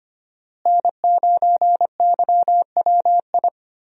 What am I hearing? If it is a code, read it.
Morse code, heard as a keyed tone, about 25 wpm, 700 Hz. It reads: N9YWI
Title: USA Call Signs - 5 Characters - Repeated 3x - 25wpm